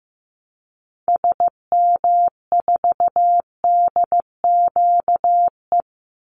Morse code, heard as a keyed tone, about 15 wpm, 700 Hz. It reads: SM4DQE
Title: Call Signs Repeated 3x - 15wpm